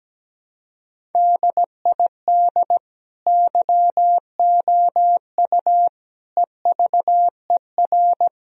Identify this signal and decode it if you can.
Morse code, heard as a keyed tone, about 17 wpm, 700 Hz. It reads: DID YOU EVER
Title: Sentences from Top 1000 Words - Review Part 1 - 17wpm